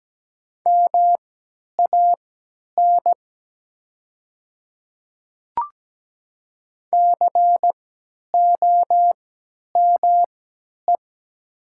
Morse code, heard as a keyed tone, about 17 wpm, spaced as for 10 wpm, 700 Hz. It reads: MAN COME
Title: 3x Character Spacing - Top 300 Words - 17wpm